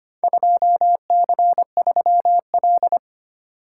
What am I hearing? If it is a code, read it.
Morse code, heard as a keyed tone, about 25 wpm, 700 Hz. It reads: 2C3L
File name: practice / 25wpm / Four Letter-Number Combinations - 25wpm